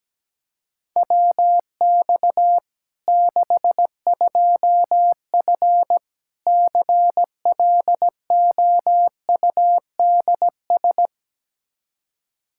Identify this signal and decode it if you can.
Morse code, heard as a keyed tone, about 17 wpm, 700 Hz. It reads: WX 62F CLOUDS
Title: K1USN SST QSO Practice 17wpm